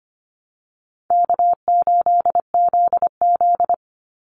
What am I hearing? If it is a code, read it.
Morse code, heard as a keyed tone, about 25 wpm, 700 Hz. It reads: K8ZZ
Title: Easy Call Signs Repeated 3x - 25wpm